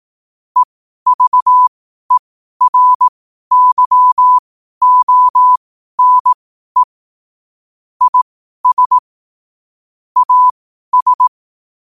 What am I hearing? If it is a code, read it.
Morse code, heard as a keyed tone, about 18 wpm, spaced as for 12 wpm, 1000 Hz. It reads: EVERYONE IS AS
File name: qotdfiles / 12wpm